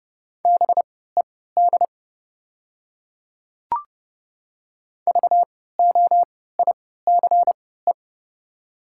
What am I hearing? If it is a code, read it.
Morse code, heard as a keyed tone, about 31 wpm, spaced as for 17 wpm, 690 Hz. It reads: BED VOICE
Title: 3x Character Spacing - Top 500 Words - 30wpm